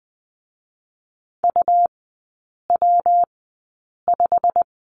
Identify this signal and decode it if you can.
Morse code, heard as a keyed tone, about 20 wpm, 700 Hz. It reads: U W 5
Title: 114 Lesson 19e - Letter w - Three Characters 20wpm